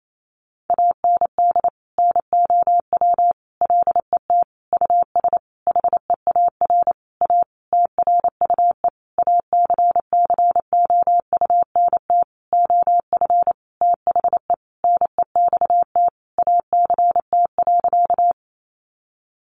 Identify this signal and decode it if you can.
Morse code, heard as a keyed tone, about 28 wpm, 700 Hz. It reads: AND NOW LET US HEAR A TRUE ACCOUNT OF THE NEXT ACT.
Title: Sentences from Top 900 Words 28wpm